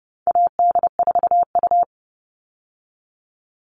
Morse code, heard as a keyed tone, about 30 wpm, 700 Hz. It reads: AD4U
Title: All Cut Numbers - 4-Digit 30wpm